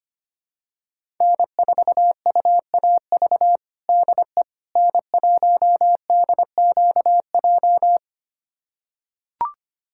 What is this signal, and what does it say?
Morse code, heard as a keyed tone, about 25 wpm, 700 Hz. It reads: N4UAV DE N1DQJ
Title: USA Callsign DE Callsign - 5 Characters - 25wpm